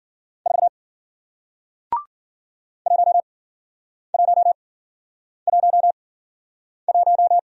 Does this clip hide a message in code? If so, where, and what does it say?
Yes: Morse code from 0.5 to 7.4 s, heard as a keyed tone, about 54 wpm, spaced as for 9 wpm, 700 Hz. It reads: 4 1111